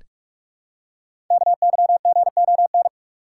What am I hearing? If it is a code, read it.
Morse code, heard as a keyed tone, about 44 wpm, 700 Hz. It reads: KYGON
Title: Code Groups - 5 Letters - 45wpm